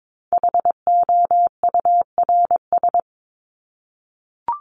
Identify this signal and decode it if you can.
Morse code, heard as a keyed tone, about 22 wpm, 700 Hz. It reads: HOURS
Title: ES:S Suffix - Top 100 Words with Base Words 22wpm